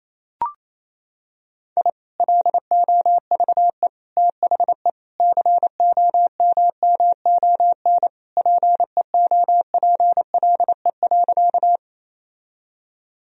Morse code, heard as a keyed tone, about 28 wpm, 700 Hz. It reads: I LOVE THE COMMON PEOPLE.
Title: Sentences from Top 1000 Words - Review Part 3 - 28wpm